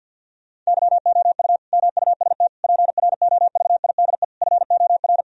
Morse code, heard as a keyed tone, about 50 wpm, 700 Hz. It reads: YOU MUST PROVIDE FOR
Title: Sentences from Top 1000 Words - Review Part 3 - 50wpm